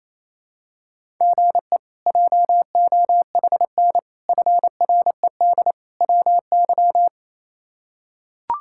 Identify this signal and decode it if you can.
Morse code, heard as a keyed tone, about 27 wpm, 700 Hz. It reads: GE JOHN FRED WY
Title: K1USN SST QSO Practice - 28wpm